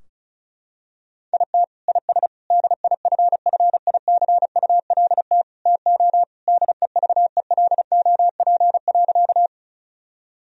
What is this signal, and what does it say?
Morse code, heard as a keyed tone, about 34 wpm, 700 Hz. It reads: IT IS DIFFICULT TO DEVELOP.
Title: Sentences from Top 1000 Words - Review Part 2 - 35wpm